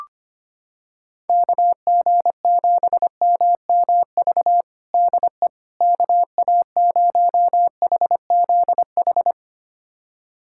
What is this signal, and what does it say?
Morse code, heard as a keyed tone, about 25 wpm, 700 Hz. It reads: KG7MMV DE KA0HZH
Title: USA Callsign DE Callsign 25wpm